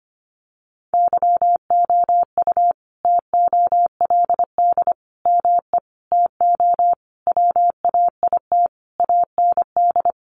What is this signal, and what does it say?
Morse code, heard as a keyed tone, about 25 wpm, 700 Hz. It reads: YOU TOLD ME TO WAIT AND
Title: Sentences from Top 1000 Words 25wpm